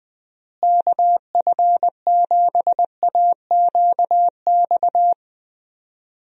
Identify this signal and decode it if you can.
Morse code, heard as a keyed tone, about 20 wpm, 700 Hz. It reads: KF7AQX
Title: USA Call Signs - 6 Characters - Repeated 3x - 20wpm